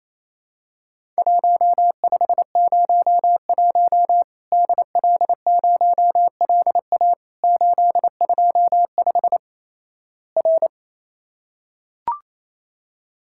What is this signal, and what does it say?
Morse code, heard as a keyed tone, about 28 wpm, 700 Hz. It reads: 1501 DL0LA 825 R
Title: Worked All Europe DX Contest - QTC Exchange - 28wpm